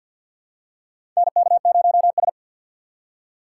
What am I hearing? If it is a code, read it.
Morse code, heard as a keyed tone, about 53 wpm, 700 Hz. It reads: NK0S
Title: USA Call Signs - 4 Characters - 6x Character Spacing - 50wpm